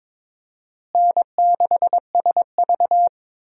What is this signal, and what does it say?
Morse code, heard as a keyed tone, about 22 wpm, 700 Hz. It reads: N6SV
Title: USA Call Signs - 4 Characters - Repeated 3x - 22wpm